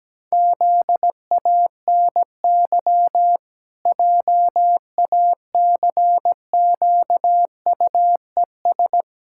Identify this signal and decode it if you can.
Morse code, heard as a keyed tone, about 17 wpm, 700 Hz. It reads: ZANY JACQUES
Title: Pangram Challenge v2 - 17wpm